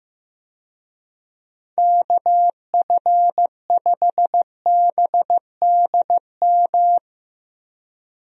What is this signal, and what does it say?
Morse code, heard as a keyed tone, about 16 wpm, 700 Hz. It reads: KF5BDM
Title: USA Call Signs - 6 Characters - 15wpm